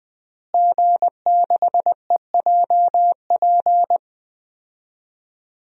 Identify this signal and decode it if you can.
Morse code, heard as a keyed tone, about 19 wpm, 700 Hz. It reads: G6EJP
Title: DXCC-100 Call Signs - 5 Characters - Repeated 2x - 20wpm